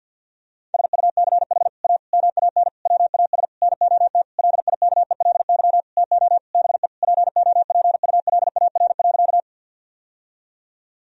Morse code, heard as a keyed tone, about 50 wpm, 700 Hz. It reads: SUCH A MAN WAS NOT LIKELY TO BE POPULAR.
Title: Sentences from Top 1000 Words - Review Part 4 - 50wpm